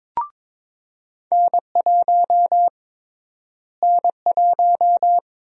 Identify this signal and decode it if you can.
Morse code, heard as a keyed tone, about 22 wpm, 700 Hz. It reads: N1 N1
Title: USA Call Sign Prefixes - Repeated 2x - 22wpm